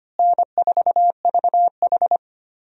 Morse code, heard as a keyed tone, about 25 wpm, 700 Hz. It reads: N4VH
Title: USA Call Signs - 4 Characters - Repeated 3x - 25wpm